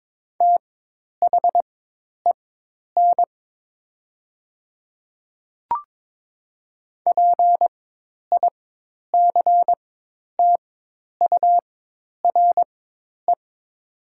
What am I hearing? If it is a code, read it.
Morse code, heard as a keyed tone, about 21 wpm, spaced as for 10 wpm, 700 Hz. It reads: THEN PICTURE
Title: 4x Character Spacing - Top 200 Words - 22wpm